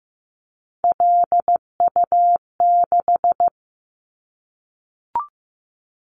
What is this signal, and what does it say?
Morse code, heard as a keyed tone, about 15 wpm, 700 Hz. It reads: LU6
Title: Callsign Prefixes for Top 100 DXCC - 15wpm